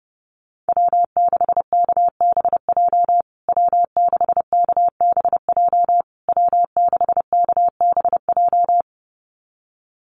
Morse code, heard as a keyed tone, about 30 wpm, 700 Hz. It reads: W6KBJ W6KBJ W6KBJ
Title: USA Call Signs Repeated 3x 30wpm